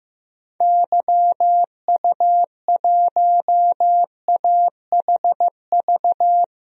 Morse code, heard as a keyed tone, about 16 wpm, 700 Hz. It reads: YU1AHV
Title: DXCC-100 Call Signs - 6 Characters - 15wpm